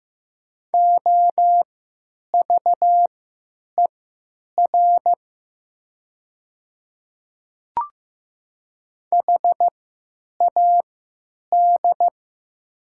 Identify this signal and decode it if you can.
Morse code, heard as a keyed tone, about 15 wpm, spaced as for 9 wpm, 700 Hz. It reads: OVER HAD
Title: 3x Character Spacing - Top 500 Words - 15wpm